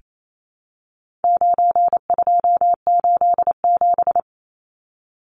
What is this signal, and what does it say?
Morse code, heard as a keyed tone, about 28 wpm, 700 Hz. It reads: 9287
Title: Numbers - 4-Digit 28wpm